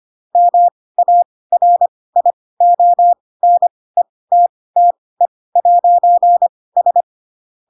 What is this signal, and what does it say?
Morse code, heard as a keyed tone, about 25 wpm, spaced as for 18 wpm, 700 Hz. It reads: MARIONETTE'S
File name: Pinocchio ch3 12 25 ik1hge